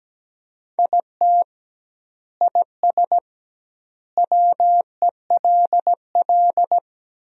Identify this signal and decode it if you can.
Morse code, heard as a keyed tone, about 17 wpm, 700 Hz. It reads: IT IS WELL
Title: 2x Word-Spacing - Sentences from Top 300 Words - 17wpm